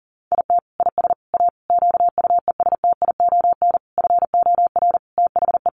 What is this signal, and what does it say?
Morse code, heard as a keyed tone, about 40 wpm, 700 Hz. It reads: IT IS A QUESTION FOR THE
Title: Sentences from Top 1300 Words 40wpm